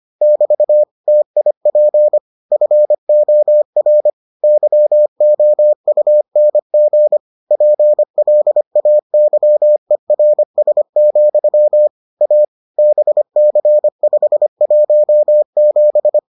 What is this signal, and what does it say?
Morse code, heard as a keyed tone, about 25 wpm, 600 Hz. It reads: <BT> TIP FOR YOUNG PLAYERS, A BC517